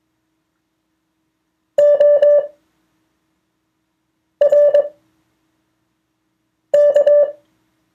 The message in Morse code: ORK